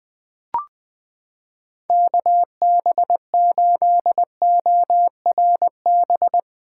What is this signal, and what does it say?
Morse code, heard as a keyed tone, about 20 wpm, 700 Hz. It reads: KB8ORB
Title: USA Callsign DE Callsign - 6 Characters - 20wpm